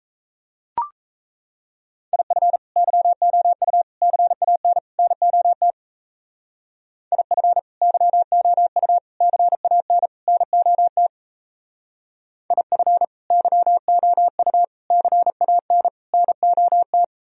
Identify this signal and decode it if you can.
Morse code, heard as a keyed tone, about 34 wpm, 700 Hz. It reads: IF YOU CAN NOT IF YOU CAN NOT IF YOU CAN NOT
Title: Speed-Racing - Sets of 4 Words 28wpm